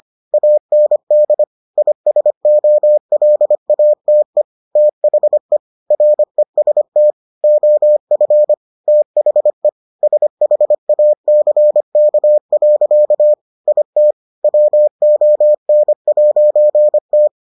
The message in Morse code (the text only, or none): AND ISOLATE THE REST OF THE SHACK. IT WON'T